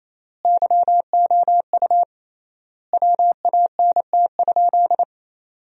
Morse code, heard as a keyed tone, about 27 wpm, 700 Hz. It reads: YOU WANT?
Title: 3x Word-Spacing - Sentences from Top 200 Words - 28wpm